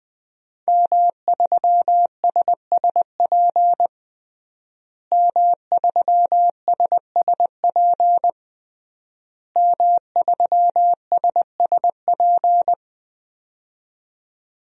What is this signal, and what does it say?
Morse code, heard as a keyed tone, about 20 wpm, 700 Hz. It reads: M3SSP M3SSP M3SSP
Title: DXCC-100 Call Signs - 5 Characters - Repeated 3x - 20wpm